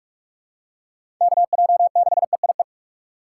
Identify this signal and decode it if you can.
Morse code, heard as a keyed tone, about 45 wpm, 700 Hz. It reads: KJ6EIE
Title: USA Call Signs - 6 Characters - 6x Character Spacing - 45wpm